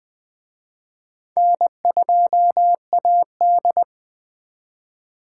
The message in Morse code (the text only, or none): N2AD